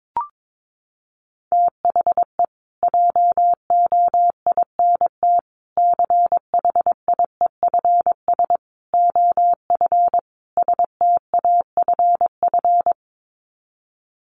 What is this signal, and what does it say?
Morse code, heard as a keyed tone, about 22 wpm, 700 Hz. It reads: THE JOINT CHIEFS OF STAFF
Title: Sets of 5 Words Encore2 22wpm